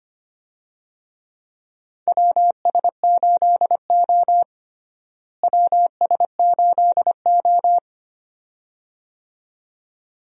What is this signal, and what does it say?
Morse code, heard as a keyed tone, about 25 wpm, 700 Hz. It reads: WS8O WS8O
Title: USA Call Signs - 4 Characters - Repeated 2x - 25wpm